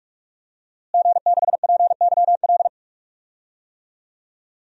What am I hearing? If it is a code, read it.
Morse code, heard as a keyed tone, about 41 wpm, 700 Hz. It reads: G6PYL
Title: DXCC-100 Call Signs - 5 Characters - Repeated 2x - 45wpm